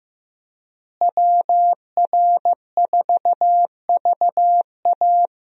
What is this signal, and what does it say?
Morse code, heard as a keyed tone, about 15 wpm, 700 Hz. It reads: WR4VA
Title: USA Call Signs - 5 Characters - Repeated 2x - 15wpm